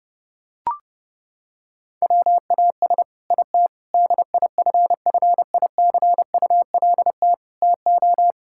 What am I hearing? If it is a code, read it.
Morse code, heard as a keyed tone, about 30 wpm, 700 Hz. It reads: WAS IT DIFFICULT TO
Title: Sentences from Top 1000 Words - Review Part 6 - 30wpm